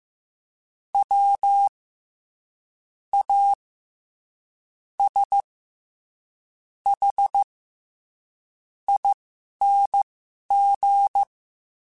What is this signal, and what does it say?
Morse code, heard as a keyed tone, about 15 wpm, spaced as for 11 wpm, 800 Hz. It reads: W A S H ING
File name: Special INGs-mp3s / Special INGs-mp3s / SlowToFaster ing @10